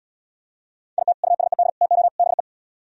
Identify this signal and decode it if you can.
Morse code, heard as a keyed tone, about 76 wpm, 690 Hz. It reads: IT USED TO BE